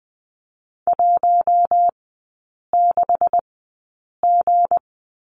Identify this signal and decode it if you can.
Morse code, heard as a keyed tone, about 20 wpm, spaced as for 8 wpm, 690 Hz. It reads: 16G
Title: 159 Lesson 26e - Letter g - Three Characters 20wpm